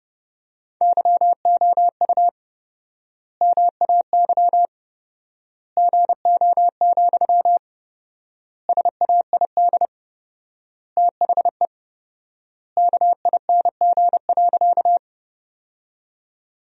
Morse code, heard as a keyed tone, about 30 wpm, 700 Hz. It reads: YOU MAY GO, SAID THE KING.
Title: 4x Word-Spacing - Sentences from Top 400 Words - 30wpm